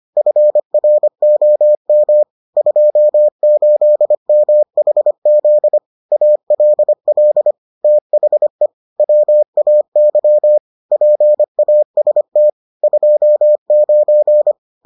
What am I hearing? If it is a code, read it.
Morse code, heard as a keyed tone, about 25 wpm, 600 Hz. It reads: FROM 28MHZ ALL THE WAY PAST 29